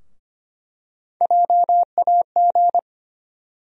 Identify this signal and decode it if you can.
Morse code, heard as a keyed tone, about 25 wpm, 700 Hz. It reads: JAG
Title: USA Call Sign Postfixes - 3 Letters 25wpm